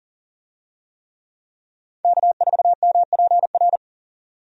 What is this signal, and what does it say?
Morse code, heard as a keyed tone, about 40 wpm, 700 Hz. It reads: K4MPR
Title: USA Call Signs - 5 Characters - 40wpm